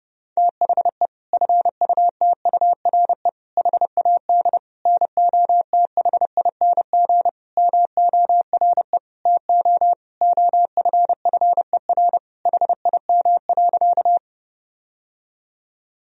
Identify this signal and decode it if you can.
Morse code, heard as a keyed tone, about 32 wpm, 700 Hz. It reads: THE FUTURE HAD NOTHING MORE TO OFFER HIM.
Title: Sentences from Top 1000 Words - Review Part 4 - 30wpm